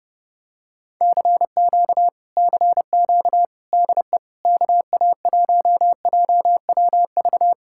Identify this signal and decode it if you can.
Morse code, heard as a keyed tone, about 32 wpm, 700 Hz. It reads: CQ CQ DE KA1JWV